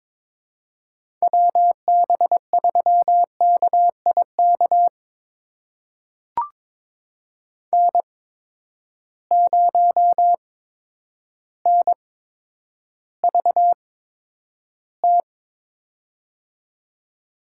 Morse code, heard as a keyed tone, about 22 wpm, spaced as for 10 wpm, 700 Hz. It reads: WB3KIK N0NVT
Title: USA Call Signs - 8x Character Spacing - 22wpm